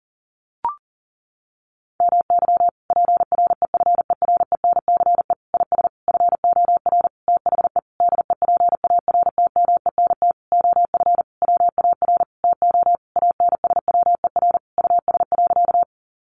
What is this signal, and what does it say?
Morse code, heard as a keyed tone, about 40 wpm, 700 Hz. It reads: MY PREFERENCE IS FOR THE DEPARTMENT OF WAR TO ANSWER US.
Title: Sentences from Top 1500 Words 40wpm